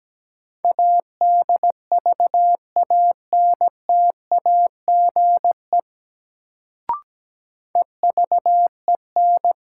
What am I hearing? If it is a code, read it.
Morse code, heard as a keyed tone, about 17 wpm, 700 Hz. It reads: ADVANTAGE EVEN
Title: Sets of 4 Words - New Word Drill - 17wpm